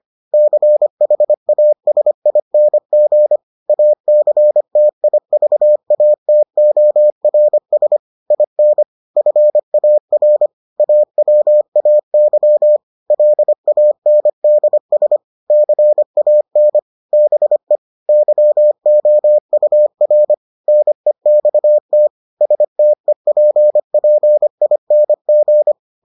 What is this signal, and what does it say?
Morse code, heard as a keyed tone, about 25 wpm, 600 Hz. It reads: CHASING ACTIVATORS IN FAR AWAY LANDS CAN BE YOUR NEXT STEPPING